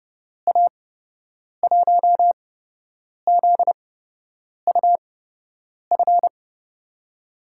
Morse code, heard as a keyed tone, about 30 wpm, spaced as for 8 wpm, 700 Hz. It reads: A1ZUF
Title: USA Call Signs - 6 Characters - 8x Character Spacing - 30wpm